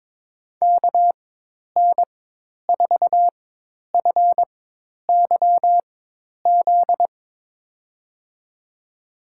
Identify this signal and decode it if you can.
Morse code, heard as a keyed tone, about 22 wpm, spaced as for 10 wpm, 700 Hz. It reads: KN4FYZ
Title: USA Call Signs - 4x Character Spacing - 22wpm